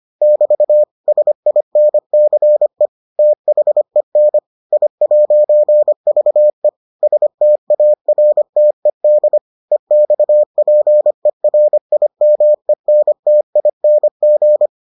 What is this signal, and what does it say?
Morse code, heard as a keyed tone, about 25 wpm, 600 Hz. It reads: <BT> SINCE THEN I'VE STARTED EXPERIMENTING